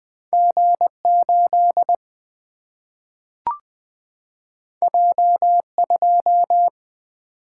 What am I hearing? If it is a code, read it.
Morse code, heard as a keyed tone, about 20 wpm, 700 Hz. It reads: G8 J2
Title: Single-Letter Single-Number 20wpm